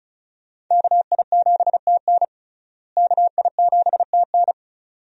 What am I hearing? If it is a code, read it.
Morse code, heard as a keyed tone, about 36 wpm, 700 Hz. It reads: KI7TN KI7TN
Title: USA Call Signs - 5 Characters - Repeated 2x - 35wpm